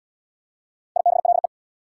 Morse code, heard as a keyed tone, about 69 wpm, 700 Hz. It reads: IBLE